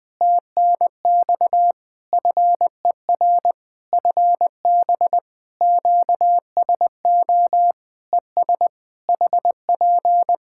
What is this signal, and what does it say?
Morse code, heard as a keyed tone, about 20 wpm, 700 Hz. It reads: TNX FER FB QSO ES HP
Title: Practice QSO - Exchange 4 - Ending - 20wpm